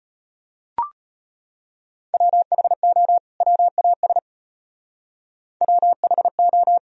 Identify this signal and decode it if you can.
Morse code, heard as a keyed tone, about 34 wpm, 700 Hz. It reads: WHO WAS WHO